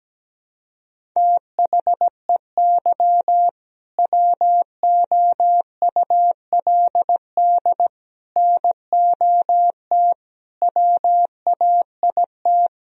Morse code, heard as a keyed tone, about 17 wpm, 700 Hz. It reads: THEY WOULD NOT WAIT
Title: Sentences from Top 1000 Words - Review Part 3 - 17wpm